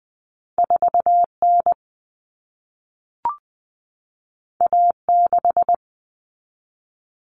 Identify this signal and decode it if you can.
Morse code, heard as a keyed tone, about 20 wpm, 700 Hz. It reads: 4N A6
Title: Common Cut Numbers - 2-Digit 20wpm